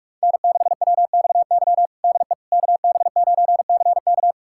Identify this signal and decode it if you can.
Morse code, heard as a keyed tone, about 42 wpm, 700 Hz. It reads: N6WXY DE KB9CK
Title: USA Callsign DE Callsign - 5 Characters - 45wpm